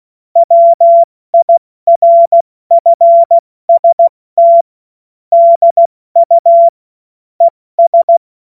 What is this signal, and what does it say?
Morse code, heard as a keyed tone, about 16 wpm, spaced as for 14 wpm, 680 Hz. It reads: WIRFST DU ES
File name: Morsekurs / Witze / Witz 01. November 2022 70 F80